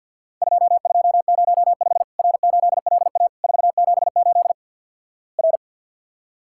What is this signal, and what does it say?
Morse code, heard as a keyed tone, about 49 wpm, 700 Hz. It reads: 2205 R8LA 478 R
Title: Worked All Europe DX Contest - QTC Exchange - 50wpm